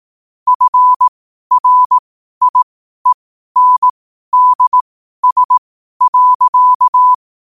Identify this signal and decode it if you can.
Morse code, heard as a keyed tone, about 18 wpm, spaced as for 12 wpm, 1000 Hz. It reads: FRIENDS.
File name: qotdfiles / 12wpm